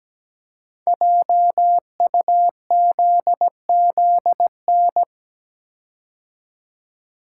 Morse code, heard as a keyed tone, about 17 wpm, 700 Hz. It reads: JUZZN
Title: Code Groups - 5 Letters - 17wpm